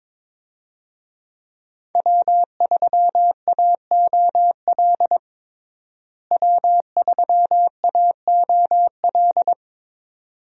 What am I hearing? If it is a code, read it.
Morse code, heard as a keyed tone, about 22 wpm, 700 Hz. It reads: W3AOL W3AOL
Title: USA Call Signs - 5 Characters - Repeated 2x - 22wpm